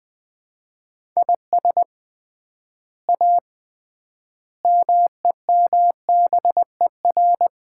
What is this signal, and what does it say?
Morse code, heard as a keyed tone, about 20 wpm, 700 Hz. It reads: IS A MEMBER
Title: 3x Word-Spacing - Sets of 5 Words - 20wpm